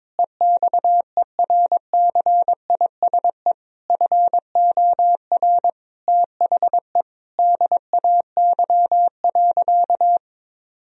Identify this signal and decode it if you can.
Morse code, heard as a keyed tone, about 22 wpm, 700 Hz. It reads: EXERCISE FOR THE DAY.
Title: Sentences from Top 1000 Words - Review Part 4 - 22wpm